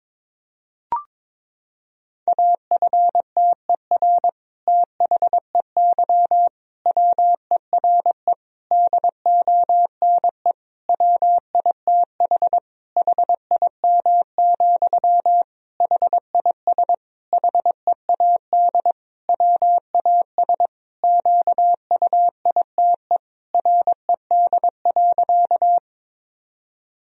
Morse code, heard as a keyed tone, about 22 wpm, 700 Hz. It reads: AFTER THEY WERE DONE WITH HIM, HIS HEAD WAS QUITE RED.
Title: Sentences from Top 1000 Words - Review Part 2 - 22wpm